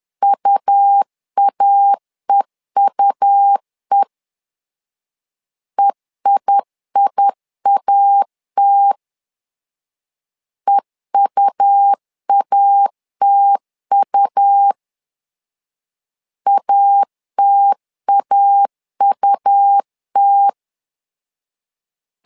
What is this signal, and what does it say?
Morse code, heard as a keyed tone, about 10 wpm, 800 Hz. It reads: UAEUE EIIAT EUATU ATAUT
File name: LEZIONI / 2 UHDN / 2 U 10wpm 1622